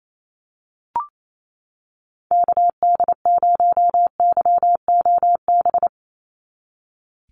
Morse code, heard as a keyed tone, about 28 wpm, 700 Hz. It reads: KD0YOB
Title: NA SOTA Activator Callsigns 28wpm